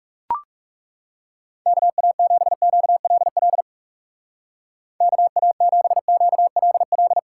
Morse code, heard as a keyed tone, about 35 wpm, 700 Hz. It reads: KA7QLL KA7QLL
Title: Speed Racing - USA 2x3 Call Signs 30wpm